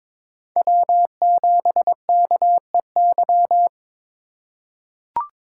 Morse code, heard as a keyed tone, about 22 wpm, 700 Hz. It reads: W7KEY
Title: USA Call Signs - 5 Characters - 4x Character Spacing - 22wpm